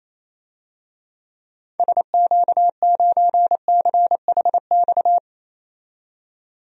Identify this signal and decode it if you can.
Morse code, heard as a keyed tone, about 28 wpm, 700 Hz. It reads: SQ9CHX
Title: DXCC-100 Call Signs - 6 Characters - 28wpm